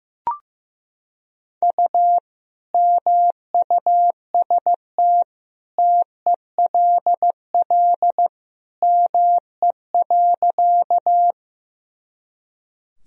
Morse code, heard as a keyed tone, about 15 wpm, 700 Hz. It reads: U MUST TELL ME.
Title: Sentences from Top 200 Words and Abbreviations 15wpm